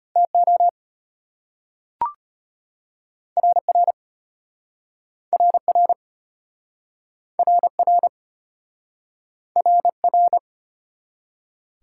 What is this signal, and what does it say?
Morse code, heard as a keyed tone, about 33 wpm, 700 Hz. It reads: TO RR RR RR RR